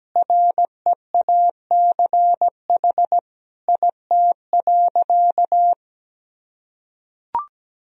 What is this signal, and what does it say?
Morse code, heard as a keyed tone, about 17 wpm, 700 Hz. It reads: REACH IT.
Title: Sentences from Top 1000 Words - Review Part 2 - 17wpm